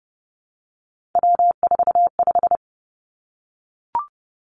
Morse code, heard as a keyed tone, about 30 wpm, 700 Hz. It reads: W45
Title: Three Challenging Letter-Number Combinations 30wpm